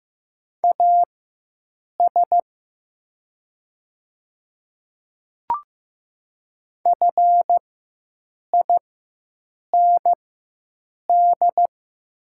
Morse code, heard as a keyed tone, about 15 wpm, spaced as for 7 wpm, 700 Hz. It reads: AS FIND
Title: 4x Character Spacing - Top 200 Words - 15wpm